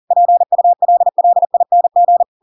Morse code, heard as a keyed tone, about 40 wpm, 700 Hz. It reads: PULLING
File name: Pinocchio / Pinocchio ch7 40 40 ik1hge